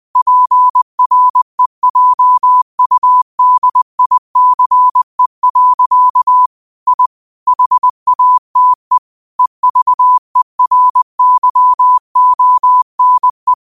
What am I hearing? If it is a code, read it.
Morse code, heard as a keyed tone, about 20 wpm, 1000 Hz. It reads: PREJUDICE. I HATE EVERYONE